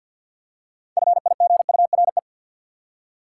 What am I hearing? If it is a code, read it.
Morse code, heard as a keyed tone, about 53 wpm, 700 Hz. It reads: FIGURE
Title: Six Letter Words - 50wpm